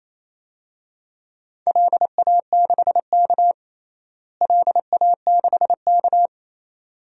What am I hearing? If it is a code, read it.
Morse code, heard as a keyed tone, about 28 wpm, 700 Hz. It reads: LA6K LA6K
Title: DXCC-100 Call Signs - 4 Characters - Repeated 2x - 28wpm